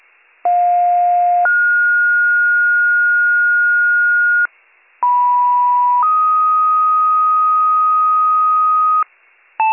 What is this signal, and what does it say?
Two-tone page, heard as a one-second tone, then a three-second tone, about 710 then 1430 Hz, about 980 then 1250 Hz, with noise. It is heard from 0.4 to 9.0 s.